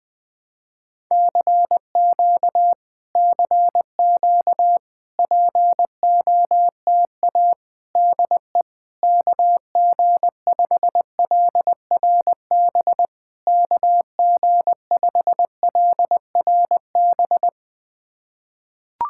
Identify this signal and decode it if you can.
Morse code, heard as a keyed tone, about 20 wpm, 700 Hz. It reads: CQ CQ POTA DE KG5LRB KG5LRB
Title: POTA QSO Practice - 20wpm